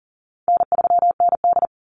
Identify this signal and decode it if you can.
Morse code, heard as a keyed tone, about 40 wpm, 700 Hz. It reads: N3ND
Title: Easy Call Signs - 40wpm